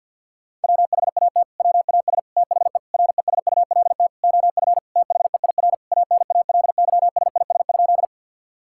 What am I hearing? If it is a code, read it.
Morse code, heard as a keyed tone, about 50 wpm, 700 Hz. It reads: WHAT WAS THE RESULT OF THEIR ANALYSIS?
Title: Sentences from Top 1000 Words - Review Part 5 - 50wpm